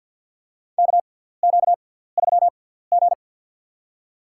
Morse code, heard as a keyed tone, about 53 wpm, spaced as for 17 wpm, 700 Hz. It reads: KQ3G
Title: USA Call Signs - 4 Characters - 6x Character Spacing - 50wpm